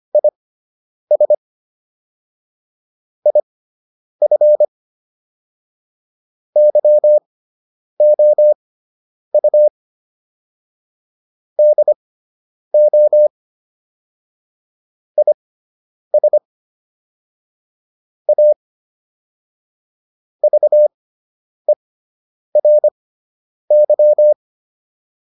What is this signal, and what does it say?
Morse code, heard as a keyed tone, about 25 wpm, spaced as for 9 wpm, 600 Hz. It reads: IS IF YOU DO IS A VERY